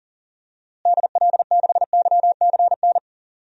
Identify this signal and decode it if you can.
Morse code, heard as a keyed tone, about 40 wpm, 700 Hz. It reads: DL6YCN
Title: DXCC-100 Call Signs - 6 Characters - Repeated 3x - 40wpm